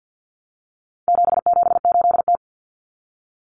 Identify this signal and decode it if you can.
Morse code, heard as a keyed tone, about 50 wpm, 700 Hz. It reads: 778T
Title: Common Cut Numbers - 4-Digit 50wpm